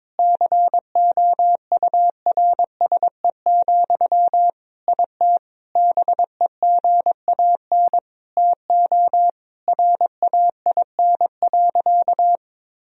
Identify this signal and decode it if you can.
Morse code, heard as a keyed tone, about 22 wpm, 700 Hz. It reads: COURSE, IT BEGAN TO RAIN.
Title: Sentences from Top 1000 Words - Review Part 1 - 22wpm